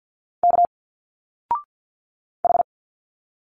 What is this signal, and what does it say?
Morse code, heard as a keyed tone, about 50 wpm, 700 Hz. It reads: K H
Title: Single Letters 50wpm